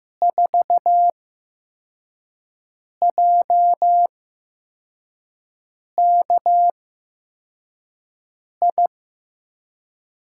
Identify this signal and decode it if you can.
Morse code, heard as a keyed tone, about 15 wpm, spaced as for 4 wpm, 700 Hz. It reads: 4JKI
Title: USA Call Signs - 5 Characters - 8x Character Spacing - 15wpm